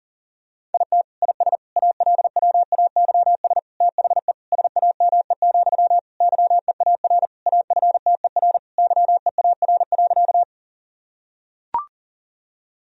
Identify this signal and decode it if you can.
Morse code, heard as a keyed tone, about 40 wpm, 700 Hz. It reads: IT IS ALWAYS THE SAME, YEAR AFTER YEAR.
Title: Sentences from Top 1000 Words - Review Part 1 - 40wpm